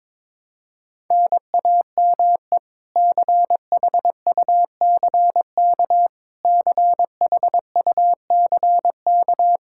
Morse code, heard as a keyed tone, about 21 wpm, 700 Hz. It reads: NAME CHUCK CHUCK
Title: K1USN SST QSO Practice 22wpm